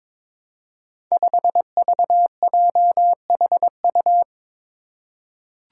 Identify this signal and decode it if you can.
Morse code, heard as a keyed tone, about 23 wpm, 700 Hz. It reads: 5VJHU
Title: Five Challenging Letter-Number Combinations 22wpm